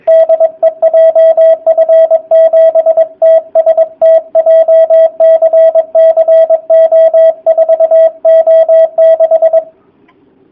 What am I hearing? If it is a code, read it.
Morse code, heard as a keyed tone, about 22 wpm, 660 Hz, with radio noise. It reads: DE JF7TST JCC O4O6
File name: jh8cbh / 2019cw / sp23- w